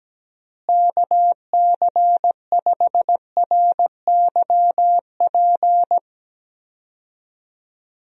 Morse code, heard as a keyed tone, about 17 wpm, 700 Hz. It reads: KC5RYP
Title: USA Call Signs - 6 Characters - Repeated 2x - 17wpm